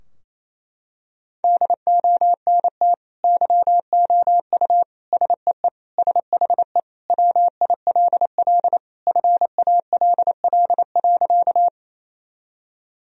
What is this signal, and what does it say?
Morse code, heard as a keyed tone, about 28 wpm, 700 Hz. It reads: DONT YOU SEE SHE WILL FALL.
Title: Sentences from Top 1000 Words - Review Part 2 - 28wpm